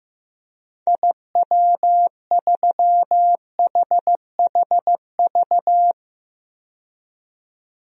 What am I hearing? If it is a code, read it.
Morse code, heard as a keyed tone, about 15 wpm, 700 Hz. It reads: IW3HHV
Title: DXCC-100 Call Signs - 6 Characters - 15wpm